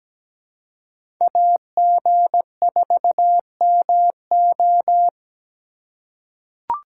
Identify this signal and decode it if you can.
Morse code, heard as a keyed tone, about 17 wpm, 700 Hz. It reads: AG4MO
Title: USA Call Signs - 5 Characters - 2x Character Spacing - 17wpm